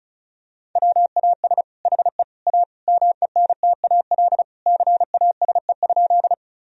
Morse code, heard as a keyed tone, about 36 wpm, 700 Hz. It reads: WAS HE A MENTAL CASE?
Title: Sentences from Top 1000 Words - Review Part 4 - 35wpm